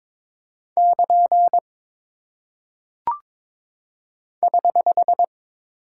Morse code, heard as a keyed tone, about 22 wpm, 700 Hz. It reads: <KN> <HH>
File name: practice / 22wpm / Single Punctuation-Prosign - 22wpm